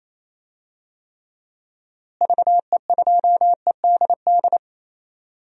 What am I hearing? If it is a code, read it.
Morse code, heard as a keyed tone, about 28 wpm, 700 Hz. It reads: VE2EDD
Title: DXCC-100 Call Signs - 6 Characters - Repeated 2x - 28wpm